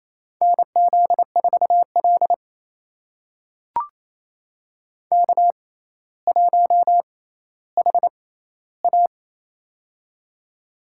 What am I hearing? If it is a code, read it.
Morse code, heard as a keyed tone, about 28 wpm, spaced as for 14 wpm, 700 Hz. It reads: NZ4L K1HA
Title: USA Call Signs - 4 Characters - 6x Character Spacing - 28wpm